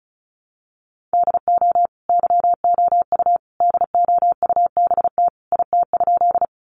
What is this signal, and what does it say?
Morse code, heard as a keyed tone, about 34 wpm, 700 Hz. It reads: DO YOU DOUBT IT?
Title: Sentences from Top 1500 Words 35wpm